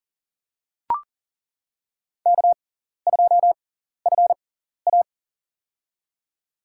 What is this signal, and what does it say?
Morse code, heard as a keyed tone, about 40 wpm, spaced as for 14 wpm, 700 Hz. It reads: K2FA
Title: USA Call Signs - 6x Character Spacing - 40wpm